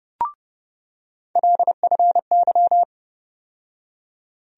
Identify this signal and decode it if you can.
Morse code, heard as a keyed tone, about 30 wpm, 700 Hz. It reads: LFY
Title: USA Call Sign Postfixes - 3 Letters 30wpm